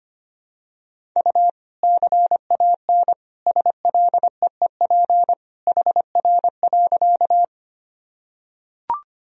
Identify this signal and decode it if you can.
Morse code, heard as a keyed tone, about 25 wpm, 700 Hz. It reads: U CAN SLEEP HR.
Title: Sentences from Top 500 Words and Abbreviations 25wpm